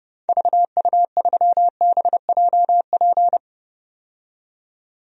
Morse code, heard as a keyed tone, about 32 wpm, 700 Hz. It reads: VU3BJP
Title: DXCC-100 Call Signs - 6 Characters - 30wpm